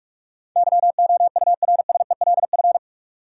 Morse code, heard as a keyed tone, about 44 wpm, 700 Hz. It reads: YOURSELF
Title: Sentences from Top 600 Words - New Word Drill - 45wpm